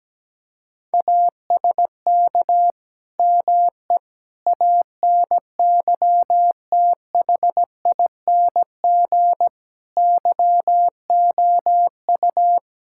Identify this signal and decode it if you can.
Morse code, heard as a keyed tone, about 17 wpm, 700 Hz. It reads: ASK ME ANYTHING YOU
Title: Sentences from Top 1600 Words 17wpm